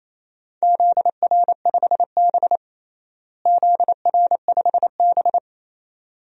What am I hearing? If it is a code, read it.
Morse code, heard as a keyed tone, about 27 wpm, 700 Hz. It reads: ZR5B ZR5B
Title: DXCC-100 Call Signs - 4 Characters - Repeated 3x - 28wpm